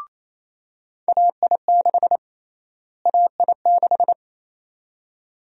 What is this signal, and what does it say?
Morse code, heard as a keyed tone, about 28 wpm, 700 Hz. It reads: AI6 AI6
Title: USA Call Sign Prefixes - Repeated 2x - 28wpm